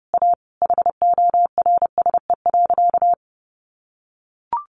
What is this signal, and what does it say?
Morse code, heard as a keyed tone, about 30 wpm, 700 Hz. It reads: A HORSE.
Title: Sentences from Top 1100 Words 30wpm